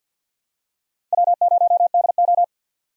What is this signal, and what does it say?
Morse code, heard as a keyed tone, about 49 wpm, 700 Hz. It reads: W0DO
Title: USA Call Signs - 4 Characters - 50wpm